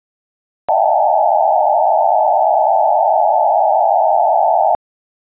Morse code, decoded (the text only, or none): none